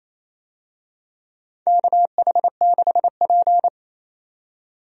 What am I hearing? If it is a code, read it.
Morse code, heard as a keyed tone, about 28 wpm, 700 Hz. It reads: KH6P
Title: DXCC-100 Call Signs - 4 Characters - 28wpm